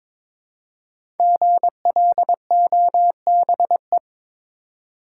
Morse code, heard as a keyed tone, about 22 wpm, 700 Hz. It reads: GLOBE
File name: 5 Letter Words - First 3 Letters Given - 22wpm